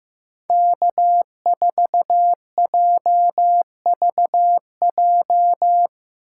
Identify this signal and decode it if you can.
Morse code, heard as a keyed tone, about 15 wpm, 700 Hz. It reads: K4JVJ
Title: USA Call Signs - 5 Characters - 8x Character Spacing - 15wpm